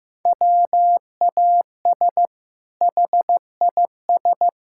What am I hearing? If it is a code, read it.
Morse code, heard as a keyed tone, about 15 wpm, 700 Hz. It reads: WAS HIS
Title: Sentences from Top 1000 Words - Review Part 5 - 15wpm